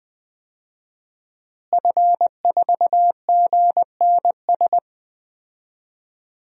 Morse code, heard as a keyed tone, about 20 wpm, 700 Hz. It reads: F4GNS
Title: DXCC-100 Call Signs - 5 Characters - Repeated 3x - 20wpm